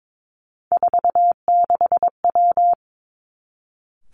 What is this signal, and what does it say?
Morse code, heard as a keyed tone, about 22 wpm, 700 Hz. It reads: 46W
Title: Three Challenging Letter-Number Combinations 22wpm